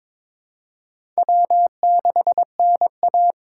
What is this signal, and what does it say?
Morse code, heard as a keyed tone, about 22 wpm, 700 Hz. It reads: W6NA
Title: USA Call Signs - 4 Characters - 2x Character Spacing - 22wpm